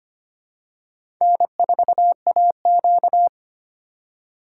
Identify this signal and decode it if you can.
Morse code, heard as a keyed tone, about 25 wpm, 700 Hz. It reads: N4AQ
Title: USA Call Signs - 4 Characters - 2x Character Spacing - 25wpm